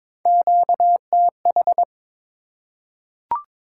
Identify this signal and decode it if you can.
Morse code, heard as a keyed tone, about 22 wpm, 690 Hz. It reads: QTH
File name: practice / 22wpm / Top 20 QSO Elements - 2x Character Spacing - 22wpm